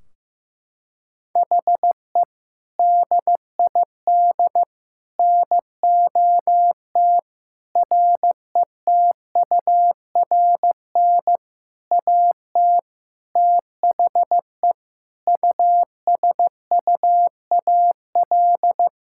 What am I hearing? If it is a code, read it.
Morse code, heard as a keyed tone, about 15 wpm, 700 Hz. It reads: HE DID NOT RETURN AT THE USUAL
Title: Sentences from Top 1600 Words 15wpm